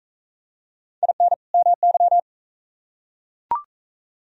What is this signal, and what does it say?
Morse code, heard as a keyed tone, about 41 wpm, 700 Hz. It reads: IN MY